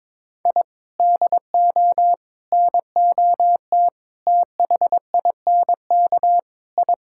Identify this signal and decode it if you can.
Morse code, heard as a keyed tone, about 22 wpm, 700 Hz. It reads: I DO NOT THINK I
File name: practice / 22wpm / Sentences with 4-Word Phrases - 22wpm